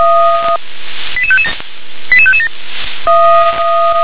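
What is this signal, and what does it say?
Five-tone selective call (ZVEI), heard as quick runs of five tones, tones of about 70 ms. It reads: 88557 88557
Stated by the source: ZVEI five-tone call received by SDR.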